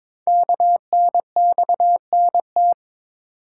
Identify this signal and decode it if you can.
Morse code, heard as a keyed tone, about 22 wpm, 700 Hz. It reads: KNXNT
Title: Code Groups - 5 Letters - 22wpm